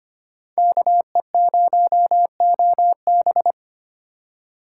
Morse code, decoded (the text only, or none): KE0OB